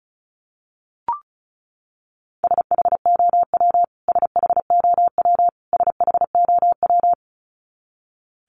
Morse code, heard as a keyed tone, about 34 wpm, 700 Hz. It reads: SHOW SHOW SHOW
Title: Top 500 Words Repeated 35wpm